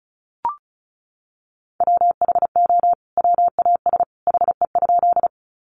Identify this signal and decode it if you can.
Morse code, heard as a keyed tone, about 34 wpm, 700 Hz. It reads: WHO WAS HE?
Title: Sentences from Top 100 Words 35wpm